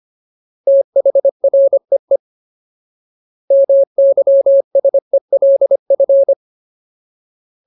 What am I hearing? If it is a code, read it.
Morse code, heard as a keyed tone, about 25 wpm, 550 Hz. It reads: THREE MYSELF
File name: Ultimate Word List / 25WPM Ultimate